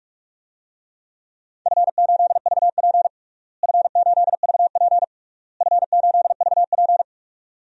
Morse code, heard as a keyed tone, about 47 wpm, 700 Hz. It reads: F8VP F8VP F8VP
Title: DXCC-100 Call Signs - 4 Characters - Repeated 3x - 45wpm